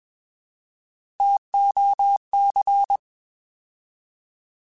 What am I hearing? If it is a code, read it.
Morse code, heard as a keyed tone, about 21 wpm, 800 Hz. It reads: TOC